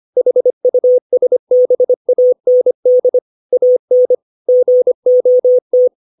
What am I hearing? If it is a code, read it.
Morse code, heard as a keyed tone, about 25 wpm, 500 Hz. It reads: HUSBAND AN GOT